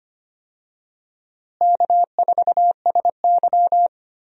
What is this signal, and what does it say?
Morse code, heard as a keyed tone, about 25 wpm, 700 Hz. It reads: K4SY
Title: USA Call Signs - 4 Characters - Repeated 2x - 25wpm